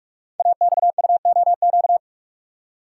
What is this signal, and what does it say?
Morse code, heard as a keyed tone, about 44 wpm, 690 Hz. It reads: AXUOQ
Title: Code Groups - 5 Letters - 45wpm